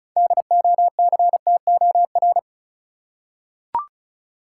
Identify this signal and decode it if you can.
Morse code, heard as a keyed tone, about 34 wpm, 700 Hz. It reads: DOCTOR
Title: Six Letter Words - 35wpm